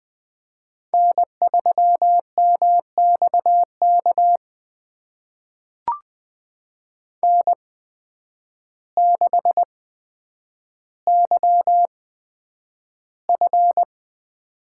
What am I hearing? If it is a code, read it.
Morse code, heard as a keyed tone, about 20 wpm, spaced as for 9 wpm, 700 Hz. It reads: N3MXK N6YF
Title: USA Call Signs - 5 Characters - 8x Character Spacing - 20wpm